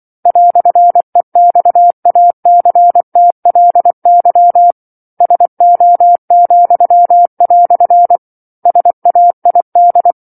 Morse code, heard as a keyed tone, about 24 wpm, 700 Hz. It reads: "EXACTLY SO," SAID